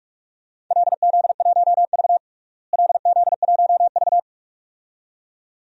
Morse code, heard as a keyed tone, about 46 wpm, 700 Hz. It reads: LZ1V LZ1V
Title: DXCC-100 Call Signs - 4 Characters - Repeated 2x - 45wpm